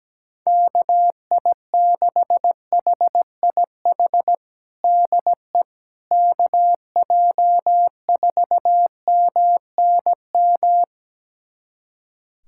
Morse code, heard as a keyed tone, about 17 wpm, 700 Hz. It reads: KI6HIH DE KJ4MNM
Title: USA Callsign DE Callsign 17wpm